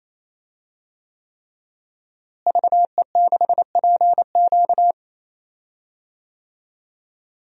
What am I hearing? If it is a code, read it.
Morse code, heard as a keyed tone, about 27 wpm, 700 Hz. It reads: VE6PQ
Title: DXCC-100 Call Signs - 5 Characters - Repeated 3x - 28wpm